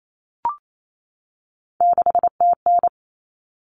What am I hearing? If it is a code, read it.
Morse code, heard as a keyed tone, about 28 wpm, 700 Hz. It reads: 6TN
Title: Common Cut Numbers - 3-Digit 28wpm